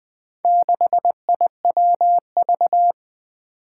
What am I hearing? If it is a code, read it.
Morse code, heard as a keyed tone, about 20 wpm, 700 Hz. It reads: 6IWV
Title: Four Challenging Letter-Number Combinations 20wpm